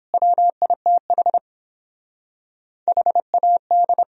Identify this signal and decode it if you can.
Morse code, heard as a keyed tone, about 29 wpm, 700 Hz. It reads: WITH HAD